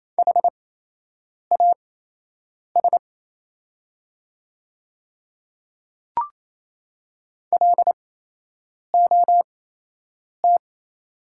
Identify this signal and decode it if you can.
Morse code, heard as a keyed tone, about 28 wpm, spaced as for 8 wpm, 700 Hz. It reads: HAS LOT
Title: 8x Character Spacing - Top 500 Words - 28wpm